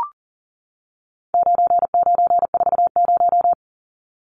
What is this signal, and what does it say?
Morse code, heard as a keyed tone, about 40 wpm, 700 Hz. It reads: 9940
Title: Numbers - 4-Digit 40wpm